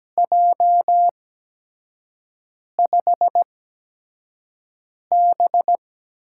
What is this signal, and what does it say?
Morse code, heard as a keyed tone, about 17 wpm, spaced as for 5 wpm, 700 Hz. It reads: J5B
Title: USA Call Signs - 4 Characters - 8x Character Spacing - 17wpm